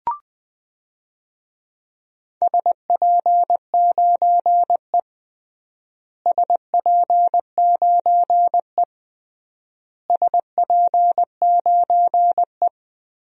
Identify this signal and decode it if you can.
Morse code, heard as a keyed tone, about 21 wpm, 700 Hz. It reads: SP9E SP9E SP9E
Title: DXCC-100 Call Signs - 4 Characters - Repeated 3x - 20wpm